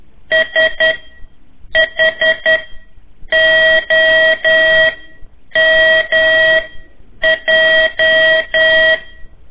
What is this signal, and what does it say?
Morse code, heard as a keyed tone, about 11 wpm, spaced as for 7 wpm, 1900 Hz. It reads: SHOMJ
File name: 3 lezione CW 1 AIR